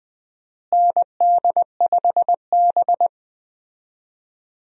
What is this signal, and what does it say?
Morse code, heard as a keyed tone, about 21 wpm, 700 Hz. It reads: ND5B
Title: USA Call Signs - 4 Characters - Repeated 3x - 20wpm